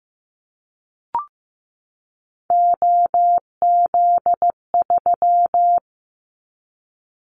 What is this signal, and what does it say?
Morse code, heard as a keyed tone, about 15 wpm, 700 Hz. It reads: OZ3
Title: Callsign Prefixes for Top 100 DXCC - 15wpm